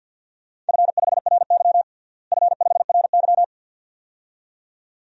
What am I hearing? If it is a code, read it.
Morse code, heard as a keyed tone, about 51 wpm, 700 Hz. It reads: F5RY F5RY
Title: DXCC-100 Call Signs - 4 Characters - Repeated 2x - 50wpm